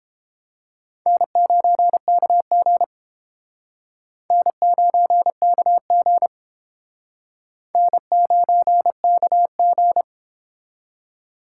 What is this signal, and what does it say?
Morse code, heard as a keyed tone, about 31 wpm, 700 Hz. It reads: N9KG N9KG N9KG